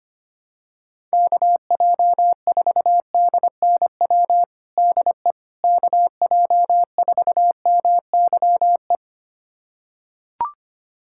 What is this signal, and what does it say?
Morse code, heard as a keyed tone, about 25 wpm, 700 Hz. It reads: KJ4DNW DE KJ4MYE
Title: USA Callsign DE Callsign - 6 Characters - 25wpm